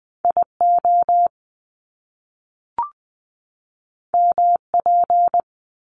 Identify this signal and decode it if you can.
Morse code, heard as a keyed tone, about 20 wpm, 700 Hz. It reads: IO MP
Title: Top Two Letter Combinations in Words 20wpm